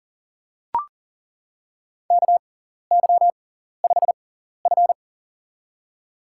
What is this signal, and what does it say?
Morse code, heard as a keyed tone, about 40 wpm, spaced as for 14 wpm, 700 Hz. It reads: KY5F
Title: USA Call Signs - 4 Characters - 6x Character Spacing - 40wpm